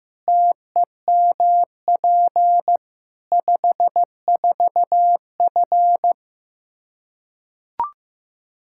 Morse code, heard as a keyed tone, about 16 wpm, 700 Hz. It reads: TEMP 54F
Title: Practice QSO - Exchange 2 - Info - 15wpm